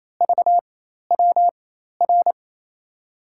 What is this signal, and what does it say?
Morse code, heard as a keyed tone, about 28 wpm, spaced as for 13 wpm, 700 Hz. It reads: VWR
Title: USA Call Signs - 6 Characters - 4x Character Spacing - 28wpm